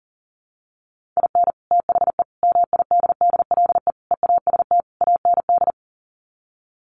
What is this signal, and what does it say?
Morse code, heard as a keyed tone, about 40 wpm, 700 Hz. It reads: IN THE MIDDLE EAST AND
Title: Sets of 5 Words Encore 40wpm